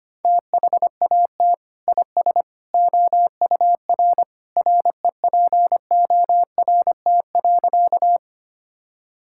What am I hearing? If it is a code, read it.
Morse code, heard as a keyed tone, about 25 wpm, 700 Hz. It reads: THAT IS OUR REPORT.
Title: Sentences from Top 1000 Words - Review Part 6 - 25wpm